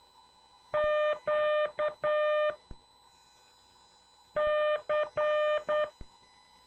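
Morse code, heard as a keyed tone, about 9 wpm, 600 Hz. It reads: Q C
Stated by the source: CW class lesson from The Martian Net with instructor N2KPS.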